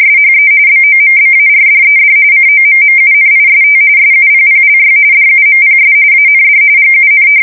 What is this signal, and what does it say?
RTTY, about 45.45 baud. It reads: SHE BOMBARDED COMMUNIST SUPPLY DUMPS AND GUN
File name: fsk170 / battleshipnj